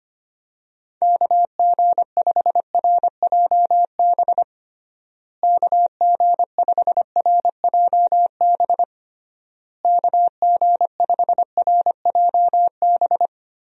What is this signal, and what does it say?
Morse code, heard as a keyed tone, about 25 wpm, 700 Hz. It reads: KG5RJB KG5RJB KG5RJB
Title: USA Call Signs - 6 Characters - Repeated 3x - 25wpm